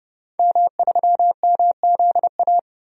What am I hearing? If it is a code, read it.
Morse code, heard as a keyed tone, about 29 wpm, 700 Hz. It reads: M3MZA
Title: DXCC-100 Call Signs - 5 Characters - 30wpm